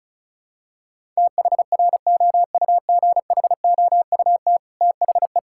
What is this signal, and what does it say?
Morse code, heard as a keyed tone, about 34 wpm, 700 Hz. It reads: THROUGHOUT THE
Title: Sentences from Top 1000 Words - Review Part 5 - 35wpm